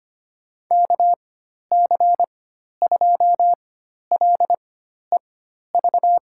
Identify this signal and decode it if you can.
Morse code, heard as a keyed tone, about 25 wpm, spaced as for 12 wpm, 700 Hz. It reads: KC2LEV
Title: USA Call Signs - 6 Characters - 4x Character Spacing - 25wpm